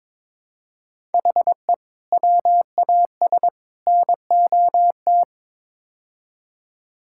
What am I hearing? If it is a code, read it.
Morse code, heard as a keyed tone, about 22 wpm, 700 Hz. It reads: HE WAS NOT
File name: practice / 22wpm / Build A Phrase - 5-Word Phrases - 22wpm